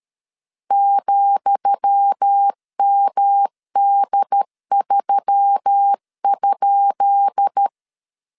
Morse code, heard as a keyed tone, about 13 wpm, 800 Hz. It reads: ,MD3?
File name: LEZIONI / 10 SIGNS / 10D COMA 12wpm 2544